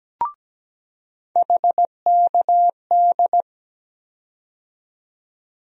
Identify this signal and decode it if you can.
Morse code, heard as a keyed tone, about 17 wpm, 700 Hz. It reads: HKD
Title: USA Call Sign Postfixes - 3 Letters 17wpm